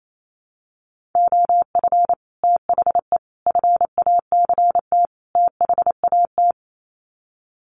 Morse code, heard as a keyed tone, about 28 wpm, 700 Hz. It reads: OF THE FACT THAT
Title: Sets of 4 Words 28wpm